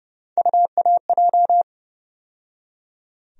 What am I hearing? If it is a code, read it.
Morse code, heard as a keyed tone, about 30 wpm, 700 Hz. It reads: UAJ
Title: USA Call Sign Postfixes - 3 Letters 30wpm